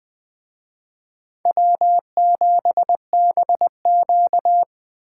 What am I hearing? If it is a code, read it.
Morse code, heard as a keyed tone, about 20 wpm, 700 Hz. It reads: W7BQ
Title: Speed Racing - USA 1x2 Call Signs 17wpm